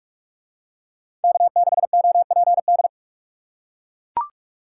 Morse code, heard as a keyed tone, about 44 wpm, 700 Hz. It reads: K6OPD